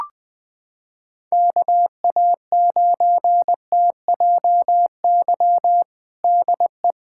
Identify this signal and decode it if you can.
Morse code, heard as a keyed tone, about 21 wpm, 700 Hz. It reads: KA9TJY DE
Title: USA Callsign DE Callsign 20wpm